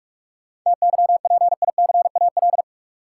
Morse code, heard as a keyed tone, about 46 wpm, 700 Hz. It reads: TYPICAL
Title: Sentences from Top 1100 Words - New Word Drill Alt - 45wpm